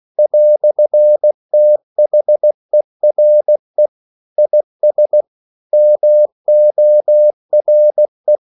Morse code, heard as a keyed tone, about 16 wpm, 600 Hz. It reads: "THERE IS MORE
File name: CW / Practice / EnglishQuotations0216